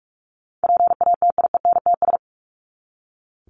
Morse code, heard as a keyed tone, about 44 wpm, 700 Hz. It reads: PATIENTS
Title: ES:S Suffix - Top 100 Words with Base Words 45wpm